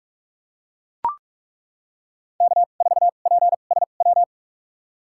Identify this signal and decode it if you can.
Morse code, heard as a keyed tone, about 41 wpm, spaced as for 31 wpm, 700 Hz. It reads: K4PSW
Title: USA Call Signs - 5 Characters - 2x Character Spacing - 45wpm